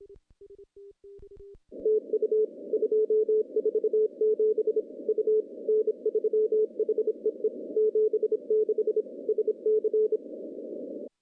Morse code, heard as a keyed tone, about 26 wpm, 430 Hz. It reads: TU 247 U N3HEE 76 SC
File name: CW-SS-k5tr